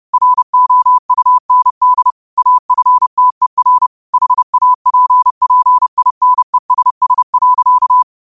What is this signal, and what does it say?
Morse code, heard as a keyed tone, about 30 wpm, 1000 Hz. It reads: ROUND AFTER HAPPINESS.
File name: qotdfiles / 30wpm